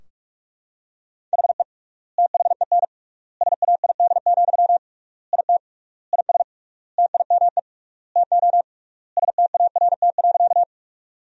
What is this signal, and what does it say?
Morse code, heard as a keyed tone, about 45 wpm, 700 Hz. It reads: HE THEN SAID, IT IS TIME TO START.
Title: 3x Word-Spacing - Sentences from Top 300 Words - 45wpm